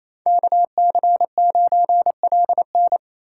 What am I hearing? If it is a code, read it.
Morse code, heard as a keyed tone, about 28 wpm, 700 Hz. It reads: KC9LN
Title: USA Call Signs - 3x Character Spacing - 28wpm